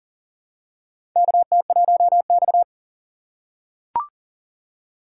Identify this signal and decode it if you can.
Morse code, heard as a keyed tone, about 40 wpm, 700 Hz. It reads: KT1X
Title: USA Call Signs - 4 Characters - 2x Character Spacing - 40wpm